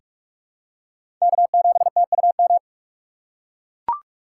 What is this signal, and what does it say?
Morse code, heard as a keyed tone, about 43 wpm, 700 Hz. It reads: K7TUM
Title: USA Call Signs - 3x Character Spacing - 45wpm